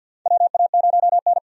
Morse code, heard as a keyed tone, about 47 wpm, 700 Hz. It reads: WA0N
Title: USA Call Signs - 4 Characters - 50wpm